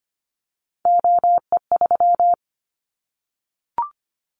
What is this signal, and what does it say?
Morse code, heard as a keyed tone, about 25 wpm, 700 Hz. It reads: OE3
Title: Callsign Prefixes for Top 50 DXCC - 25wpm